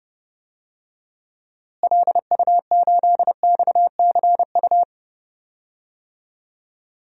Morse code, heard as a keyed tone, about 30 wpm, 700 Hz. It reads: LU8XCU
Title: DXCC-100 Call Signs - 6 Characters - Repeated 2x - 30wpm